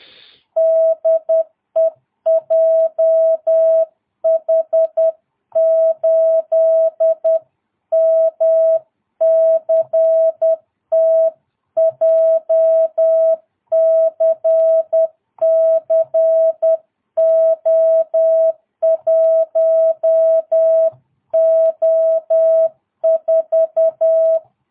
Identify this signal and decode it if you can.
Morse code, heard as a keyed tone, about 10 wpm, 650 Hz. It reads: DE JH8MCT JCC O1O4
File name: jh8cbh / 2022cw